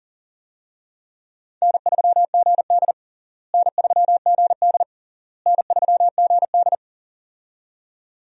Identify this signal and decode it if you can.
Morse code, heard as a keyed tone, about 40 wpm, 700 Hz. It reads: N3GD N3GD N3GD
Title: USA Call Signs - 4 Characters - Repeated 3x - 40wpm